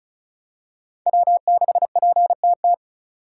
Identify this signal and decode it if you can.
Morse code, heard as a keyed tone, about 36 wpm, 700 Hz. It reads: W6PTT
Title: USA Call Signs - 5 Characters - 4x Character Spacing - 35wpm